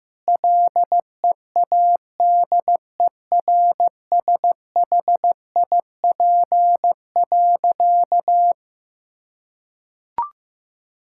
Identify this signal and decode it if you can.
Morse code, heard as a keyed tone, about 15 wpm, 700 Hz. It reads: LEADERSHIP.
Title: Sentences from Top 1600 Words 15wpm